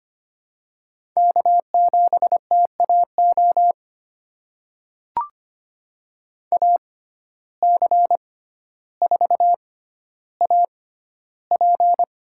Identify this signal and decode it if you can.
Morse code, heard as a keyed tone, about 25 wpm, spaced as for 13 wpm, 700 Hz. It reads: K7TAO AC4AP
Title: USA Call Signs - 5 Characters - 6x Character Spacing - 25wpm